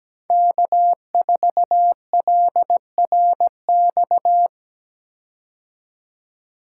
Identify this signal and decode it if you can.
Morse code, heard as a keyed tone, about 17 wpm, 700 Hz. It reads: K4LRX
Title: K1USN SST QSO Practice - 17wpm